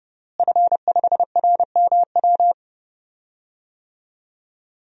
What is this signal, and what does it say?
Morse code, heard as a keyed tone, about 30 wpm, 700 Hz. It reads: F5RMW
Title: DXCC-100 Call Signs - 5 Characters - Repeated 3x - 30wpm